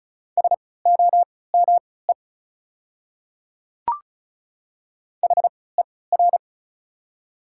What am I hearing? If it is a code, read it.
Morse code, heard as a keyed tone, about 33 wpm, spaced as for 19 wpm, 690 Hz. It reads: SOME HER
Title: 3x Character Spacing - Top 100 Words - 35wpm